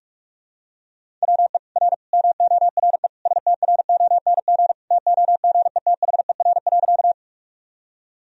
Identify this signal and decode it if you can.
Morse code, heard as a keyed tone, about 46 wpm, 690 Hz. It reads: WE R MORE STRONG TOGETHER.
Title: Sentences from Top 500 Words and Abbreviations 45wpm